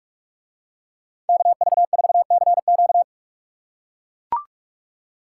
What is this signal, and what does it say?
Morse code, heard as a keyed tone, about 44 wpm, 700 Hz. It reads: KV4CQ